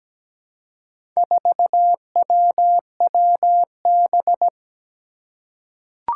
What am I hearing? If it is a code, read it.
Morse code, heard as a keyed tone, about 17 wpm, 700 Hz. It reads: 4WWB
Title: Four Challenging Letter-Number Combinations 17wpm